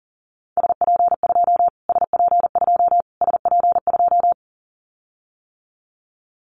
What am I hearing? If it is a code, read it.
Morse code, heard as a keyed tone, about 40 wpm, 700 Hz. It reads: SP2 SP2 SP2
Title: Callsign Prefixes for Top 25 DXCC - Repeated 3x - 40wpm